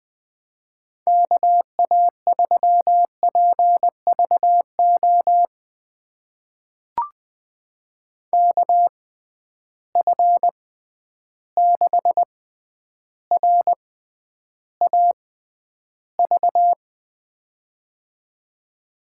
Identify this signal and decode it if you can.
Morse code, heard as a keyed tone, about 20 wpm, spaced as for 10 wpm, 700 Hz. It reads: KA3PVO KF6RAV
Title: USA Call Signs - 6 Characters - 6x Character Spacing - 20wpm